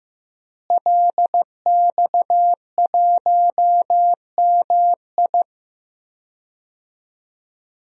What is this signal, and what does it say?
Morse code, heard as a keyed tone, about 15 wpm, 700 Hz. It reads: LX1MI
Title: DXCC-100 Call Signs - 5 Characters - Repeated 2x - 15wpm